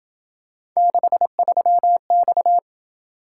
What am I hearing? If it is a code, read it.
Morse code, heard as a keyed tone, about 26 wpm, 700 Hz. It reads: 63X